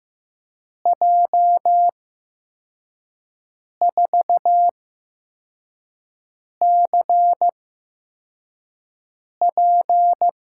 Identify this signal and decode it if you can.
Morse code, heard as a keyed tone, about 15 wpm, spaced as for 4 wpm, 700 Hz. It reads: J4CP
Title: USA Call Signs - 5 Characters - 8x Character Spacing - 15wpm